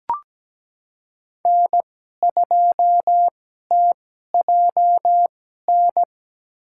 Morse code, heard as a keyed tone, about 17 wpm, spaced as for 12 wpm, 700 Hz. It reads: N2TJN
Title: USA Call Signs - 5 Characters - 2x Character Spacing - 17wpm